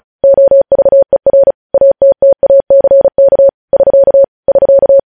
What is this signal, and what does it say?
Morse code, heard as a keyed tone, about 36 wpm, 550 Hz. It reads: OVER ATTACK <SK> <SK>
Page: news headlines in morse code at 35 wpm